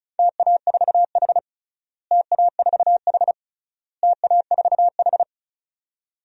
Morse code, heard as a keyed tone, about 36 wpm, 700 Hz. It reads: TA4H TA4H TA4H
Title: DXCC-100 Call Signs - 4 Characters - Repeated 3x - 35wpm